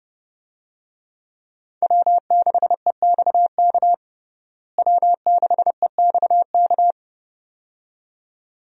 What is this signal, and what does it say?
Morse code, heard as a keyed tone, about 32 wpm, 700 Hz. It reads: W6EXK W6EXK
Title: USA Call Signs - 5 Characters - Repeated 2x - 30wpm